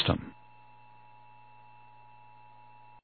Emergency Alert System (SAME) tones (the 853 and 960 Hz attention tone). It is heard from 0.5 to 3.0 s.
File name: eas / KWBE / KWBE